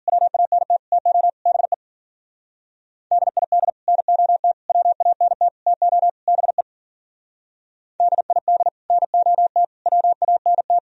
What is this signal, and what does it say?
Morse code, heard as a keyed tone, about 47 wpm, 700 Hz. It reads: WANT TO BE DID NOT WANT TO BE DID NOT WANT
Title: Speed-Racing - Sets of 5 Words 40wpm